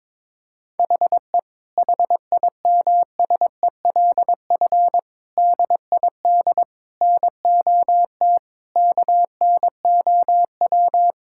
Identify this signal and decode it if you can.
Morse code, heard as a keyed tone, about 21 wpm, 700 Hz. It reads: HE HIMSELF DID NOT KNOW
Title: Sentences from Top 1600 Words 22wpm